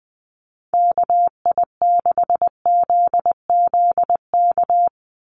Morse code, heard as a keyed tone, about 20 wpm, 700 Hz. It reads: KI6ZZK
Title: USA Call Signs 20wpm